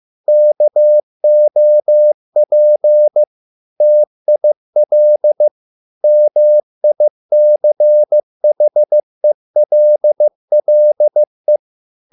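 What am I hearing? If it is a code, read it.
Morse code, heard as a keyed tone, about 15 wpm, 600 Hz. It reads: KOP TIL MICHELLE
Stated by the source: Morse code at a slow 15 wpm from CW med Gnister.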